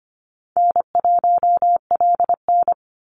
Morse code, heard as a keyed tone, about 25 wpm, 700 Hz. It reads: N1LN
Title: Easy Call Signs - 25wpm